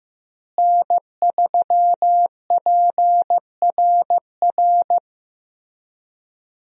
Morse code, heard as a keyed tone, about 16 wpm, 700 Hz. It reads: N3PRR
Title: USA Call Signs - 5 Characters - Repeated 3x - 15wpm